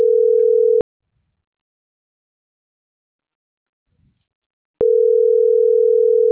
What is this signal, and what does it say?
Ringback tone (North America), 440 + 480 Hz.